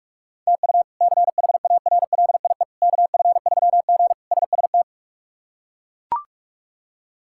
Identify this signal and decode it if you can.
Morse code, heard as a keyed tone, about 42 wpm, 700 Hz. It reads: TU CHARLIE KF3G SST
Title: K1USN SST QSO Practice - 45wpm